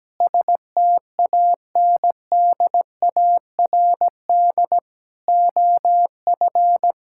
Morse code, heard as a keyed tone, about 17 wpm, 700 Hz. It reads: STANDARD OF
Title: Sentences from Top 1600 Words 17wpm